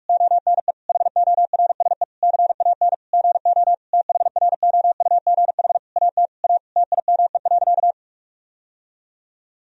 Morse code, heard as a keyed tone, about 41 wpm, 700 Hz. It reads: ONE HORSE CAN GO THROUGH AT A TIME.
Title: Sentences from Top 1000 Words - Review Part 1 - 45wpm